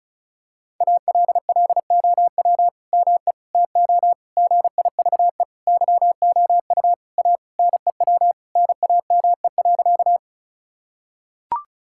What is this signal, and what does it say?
Morse code, heard as a keyed tone, about 36 wpm, 700 Hz. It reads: ALLOW ME TO GIVE YOU A NEW NAME.
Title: Sentences from Top 1000 Words - Review Part 5 - 35wpm